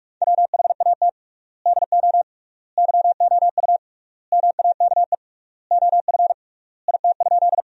Morse code, heard as a keyed tone, about 46 wpm, 690 Hz. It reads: WHAT DO YOU MAKE OF IT?
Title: 3x Word-Spacing - Sentences from Top 100 Words - 45wpm